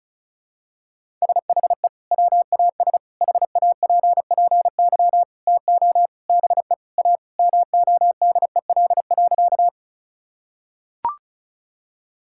Morse code, heard as a keyed tone, about 36 wpm, 700 Hz. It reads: SHE WAS HAPPY TO BE A MODEL.
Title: Sentences from Top 1000 Words - Review Part 5 - 35wpm